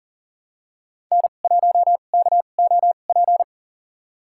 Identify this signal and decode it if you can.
Morse code, heard as a keyed tone, about 40 wpm, spaced as for 29 wpm, 700 Hz. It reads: N1KOP
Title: USA Call Signs - 5 Characters - 2x Character Spacing - 40wpm